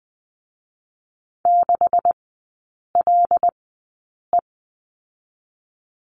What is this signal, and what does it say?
Morse code, heard as a keyed tone, about 20 wpm, spaced as for 8 wpm, 700 Hz. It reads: 6LE
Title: 152 Lesson 25e - Letter p - Three Characters 20wpm